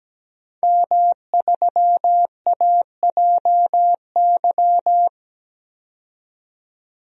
Morse code, heard as a keyed tone, about 17 wpm, 700 Hz. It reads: M3AJY
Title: DXCC-100 Call Signs - 5 Characters - 17wpm